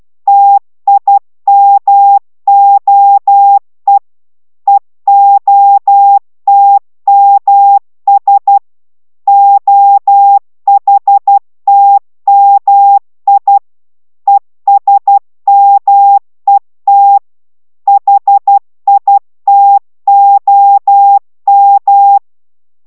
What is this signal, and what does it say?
Morse code, heard as a keyed tone, about 12 wpm, 800 Hz. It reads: TIMOE EOTMS OHTMI ESMET HITOM